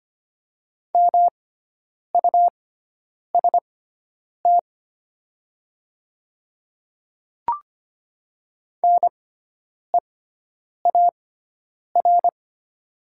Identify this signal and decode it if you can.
Morse code, heard as a keyed tone, about 25 wpm, spaced as for 9 wpm, 700 Hz. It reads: MUST NEAR
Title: 6x Character Spacing - Top 300 Words - 25wpm